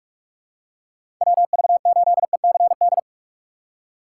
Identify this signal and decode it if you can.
Morse code, heard as a keyed tone, about 46 wpm, 700 Hz. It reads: WV8ECD
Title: USA Call Signs - 6 Characters - 2x Character Spacing - 45wpm